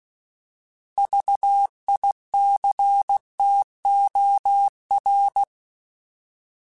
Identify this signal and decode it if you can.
Morse code, heard as a keyed tone, about 16 wpm, 800 Hz. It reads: VICTOR